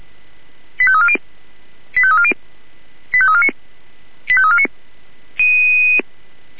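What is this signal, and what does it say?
Five-tone selective call (ZVEI), heard as quick runs of five tones, tones of about 70 ms. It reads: 85340 85340 85349 85349
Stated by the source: ZVEI five-tone call received by SDR.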